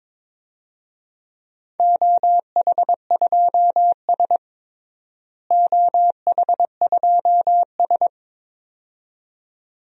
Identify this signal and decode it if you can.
Morse code, heard as a keyed tone, about 21 wpm, 700 Hz. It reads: OH2S OH2S
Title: DXCC-100 Call Signs - 4 Characters - Repeated 2x - 22wpm